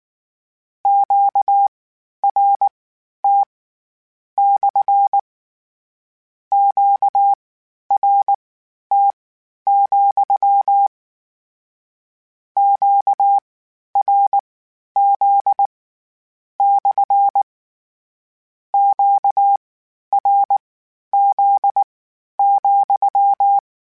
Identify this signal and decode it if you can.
Morse code, heard as a keyed tone, about 19 wpm, spaced as for 11 wpm, 800 Hz. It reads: QRT / QRT, QRZ / QRZ,
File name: Q Calls / Q Calls-mp3s / Q CALLS SPSL @18